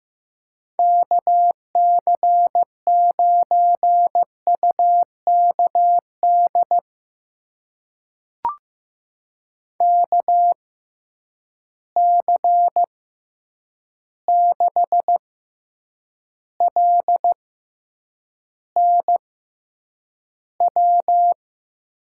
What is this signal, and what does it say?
Morse code, heard as a keyed tone, about 15 wpm, spaced as for 8 wpm, 700 Hz. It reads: KC9UKD KC6LNW
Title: USA Call Signs - 6 Characters - 6x Character Spacing - 15wpm